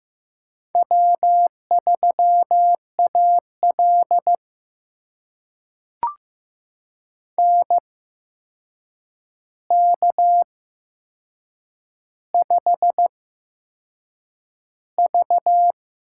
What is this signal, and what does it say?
Morse code, heard as a keyed tone, about 15 wpm, spaced as for 6 wpm, 700 Hz. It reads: W3AL NK5V
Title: USA Call Signs - 4 Characters - 8x Character Spacing - 15wpm